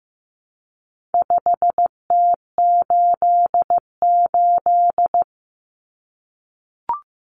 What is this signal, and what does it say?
Morse code, heard as a keyed tone, about 15 wpm, 700 Hz. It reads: 5T88
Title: Common Cut Numbers - 4-Digit 15wpm